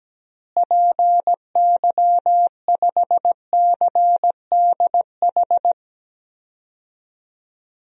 Morse code, heard as a keyed tone, about 17 wpm, 700 Hz. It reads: PY5CDH
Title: DXCC-100 Call Signs - 6 Characters - Repeated 3x - 17wpm